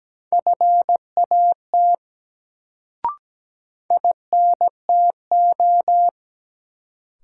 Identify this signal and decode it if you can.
Morse code, heard as a keyed tone, about 17 wpm, 700 Hz. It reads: FAT INTO
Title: Sentences from Top 600 Words - New Word Drill - 17wpm